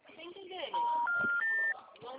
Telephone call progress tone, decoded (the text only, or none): special information tone